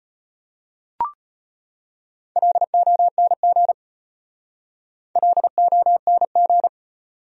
Morse code, heard as a keyed tone, about 31 wpm, 700 Hz. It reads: LONG LONG
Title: Speed-Racing Top 100 Words 25wpm